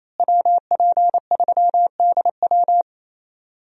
Morse code, heard as a keyed tone, about 28 wpm, 700 Hz. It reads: WP3DW
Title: USA Call Signs - 5 Characters - Repeated 2x - 28wpm